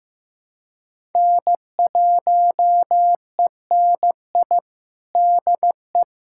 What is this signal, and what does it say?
Morse code, heard as a keyed tone, about 15 wpm, 700 Hz. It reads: N1ENI DE
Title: USA Callsign DE Callsign 15wpm